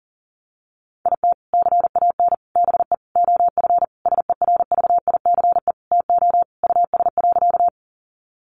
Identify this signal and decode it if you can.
Morse code, heard as a keyed tone, about 40 wpm, 700 Hz. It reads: IT CAN BE OF SERVICE TO US.
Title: Sentences from Top 1200 Words 40wpm